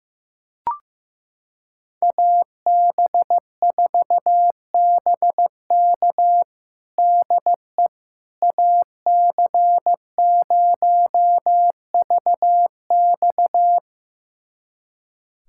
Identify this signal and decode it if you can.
Morse code, heard as a keyed tone, about 15 wpm, 700 Hz. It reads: AB4BK DE AC0VX
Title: USA Callsign DE Callsign - 5 Characters - 15wpm